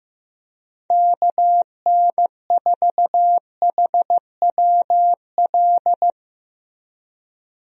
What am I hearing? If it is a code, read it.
Morse code, heard as a keyed tone, about 15 wpm, 700 Hz. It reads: KN4HWL
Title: USA Call Signs - 6 Characters - Repeated 3x - 15wpm